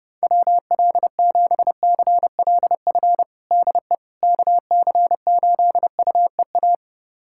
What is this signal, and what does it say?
Morse code, heard as a keyed tone, about 30 wpm, 700 Hz. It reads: WL7CLF DE KC8UEA
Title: USA Callsign DE Callsign - 6 Characters - 30wpm